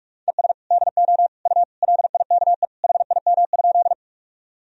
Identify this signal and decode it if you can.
Morse code, heard as a keyed tone, about 41 wpm, 700 Hz. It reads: ES DO U LIKE HIM?
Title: Sentences from Top 100 Words and Abbreviations 45wpm